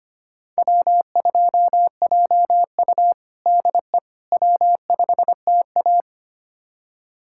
Morse code, heard as a keyed tone, about 25 wpm, 700 Hz. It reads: W2JU DE W5TA
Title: USA Callsign DE Callsign - 4 Characters - 25wpm